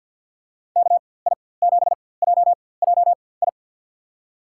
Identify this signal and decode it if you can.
Morse code, heard as a keyed tone, about 47 wpm, spaced as for 23 wpm, 700 Hz. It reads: KI7JJI
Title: USA Call Signs - 6 Characters - 4x Character Spacing - 50wpm